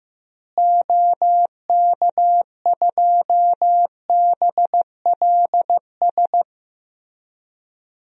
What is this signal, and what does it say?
Morse code, heard as a keyed tone, about 15 wpm, 700 Hz. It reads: OK2BLS
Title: DXCC-100 Call Signs - 6 Characters - Repeated 2x - 15wpm